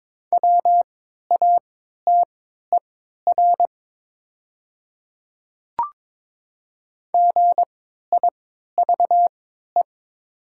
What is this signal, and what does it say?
Morse code, heard as a keyed tone, about 23 wpm, spaced as for 13 wpm, 700 Hz. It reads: WATER GIVE
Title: 3x Character Spacing - Top 200 Words - 22wpm